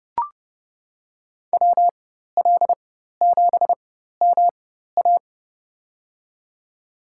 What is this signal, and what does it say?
Morse code, heard as a keyed tone, about 30 wpm, spaced as for 14 wpm, 700 Hz. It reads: WL7MA
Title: USA Call Signs - 5 Characters - 4x Character Spacing - 30wpm